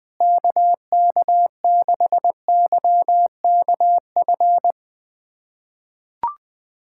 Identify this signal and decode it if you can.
Morse code, heard as a keyed tone, about 20 wpm, 700 Hz. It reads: KK6YKF
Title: USA Callsign DE Callsign 20wpm